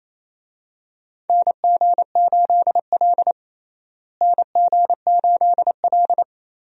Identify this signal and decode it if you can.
Morse code, heard as a keyed tone, about 28 wpm, 700 Hz. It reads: NG8L NG8L
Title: USA Call Signs - 4 Characters - Repeated 3x - 28wpm